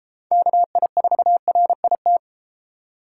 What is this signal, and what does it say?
Morse code, heard as a keyed tone, about 32 wpm, 700 Hz. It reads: KI4RIT